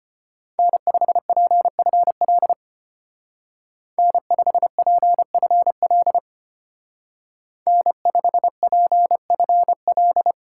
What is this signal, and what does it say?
Morse code, heard as a keyed tone, about 31 wpm, 700 Hz. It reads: N5PFL N5PFL N5PFL
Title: Speed Racing - USA 1x3 Call Signs 25wpm